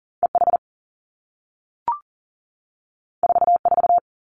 Morse code, heard as a keyed tone, about 39 wpm, 700 Hz. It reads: EH 44
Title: Two Challenging Letter-Number Combinations 40wpm